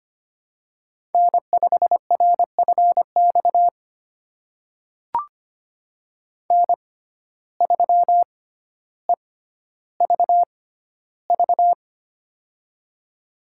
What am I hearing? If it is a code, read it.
Morse code, heard as a keyed tone, about 25 wpm, spaced as for 13 wpm, 700 Hz. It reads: N5RFX N3EVV
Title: USA Call Signs - 5 Characters - 6x Character Spacing - 25wpm